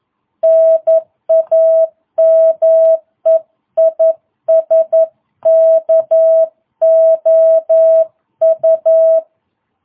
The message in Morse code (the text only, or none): NAME IS KOU